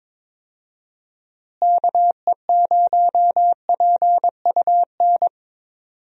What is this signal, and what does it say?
Morse code, heard as a keyed tone, about 21 wpm, 700 Hz. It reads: KE0PUN
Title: USA Call Signs - 6 Characters - Repeated 2x - 22wpm